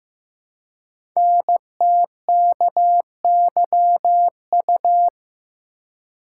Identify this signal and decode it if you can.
Morse code, heard as a keyed tone, about 16 wpm, 700 Hz. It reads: NTKYU
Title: Code Groups - 5 Letters - 15wpm